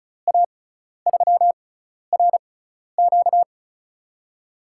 Morse code, heard as a keyed tone, about 36 wpm, spaced as for 12 wpm, 700 Hz. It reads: A3RQ
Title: USA Call Signs - 5 Characters - 6x Character Spacing - 35wpm